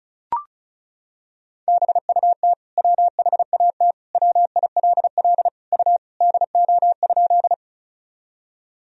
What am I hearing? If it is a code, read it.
Morse code, heard as a keyed tone, about 36 wpm, 700 Hz. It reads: BUT WHAT WILL U DO?
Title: Sentences from Top 100 Words and Abbreviations 35wpm